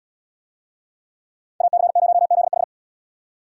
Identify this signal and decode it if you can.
Morse code, heard as a keyed tone, about 78 wpm, 700 Hz. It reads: AC0ZH